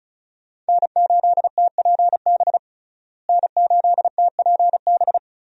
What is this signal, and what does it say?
Morse code, heard as a keyed tone, about 34 wpm, 700 Hz. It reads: N8TPB N8TPB
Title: USA Call Signs - 5 Characters - Repeated 3x - 35wpm